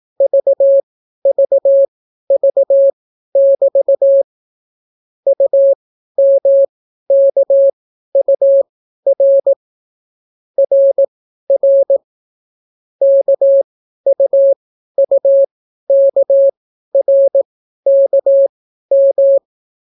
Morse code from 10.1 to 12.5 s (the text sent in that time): RR